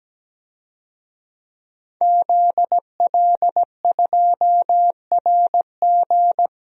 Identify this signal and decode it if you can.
Morse code, heard as a keyed tone, about 17 wpm, 700 Hz. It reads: ZL2RG
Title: DXCC-100 Call Signs - 5 Characters - Repeated 3x - 17wpm